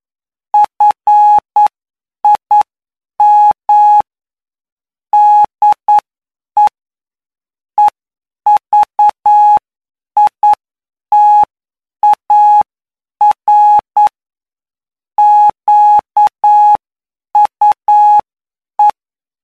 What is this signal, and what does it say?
Morse code, heard as a keyed tone, about 9 wpm, spaced as for 8 wpm, 840 Hz. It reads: FIM DE EVITAR QUE
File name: cw / py2lcd / 20-PY2LCD-Li��o 29